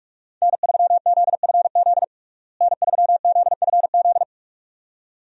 Morse code, heard as a keyed tone, about 41 wpm, 700 Hz. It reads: N3ZFZ N3ZFZ
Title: USA Call Signs - 5 Characters - Repeated 2x - 45wpm